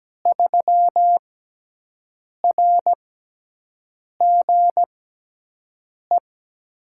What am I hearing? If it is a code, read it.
Morse code, heard as a keyed tone, about 17 wpm, spaced as for 6 wpm, 700 Hz. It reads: 3RGE
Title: USA Call Signs - 6x Character Spacing - 17wpm